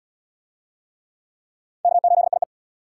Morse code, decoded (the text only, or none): K8IE